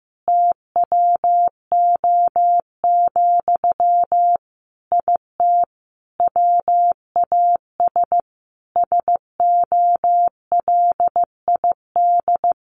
Morse code, heard as a keyed tone, about 15 wpm, 700 Hz. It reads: TWO, IT WAS SOLID
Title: Sentences from Top 1500 Words 15wpm